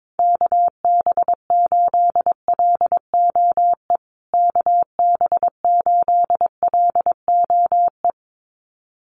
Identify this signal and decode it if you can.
Morse code, heard as a keyed tone, about 22 wpm, 700 Hz. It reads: KB8LOE KB8LOE
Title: USA Call Signs Repeated 2x 22wpm